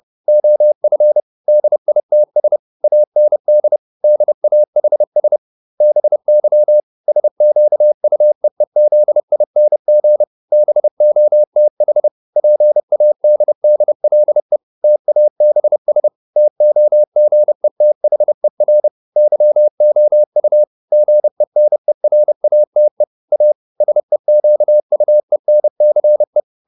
Morse code, heard as a keyed tone, about 30 wpm, 600 Hz. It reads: OF DITS AND DAHS BY SQUEEZING BOTH PADDLE TABS TOGETHER YOU GENERATE A SEQUENCE